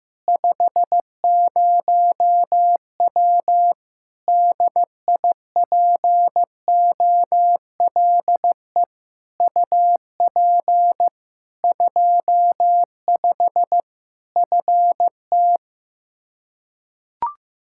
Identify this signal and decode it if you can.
Morse code, heard as a keyed tone, about 15 wpm, 700 Hz. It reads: 50W DIPOLE UP 25 FT
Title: Practice QSO - Exchange 2 - Info - 15wpm